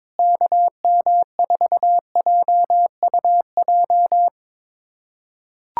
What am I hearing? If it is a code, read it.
Morse code, heard as a keyed tone, about 22 wpm, 700 Hz. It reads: KM4JUJ
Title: USA Callsign DE Callsign - 6 Characters - 22wpm